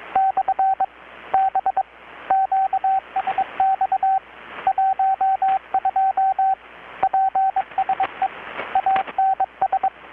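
Morse code, heard as a keyed tone, about 21 wpm, 770 Hz, with radio noise. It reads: / B QSX 12 PSE ANS
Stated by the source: Morse from KPH's Night of Nights XX maritime radiotelegraphy event, received in Chiapas, Mexico about 3,800 km away on a Kenwood R-600 with a grounded 10 m long-wire antenna.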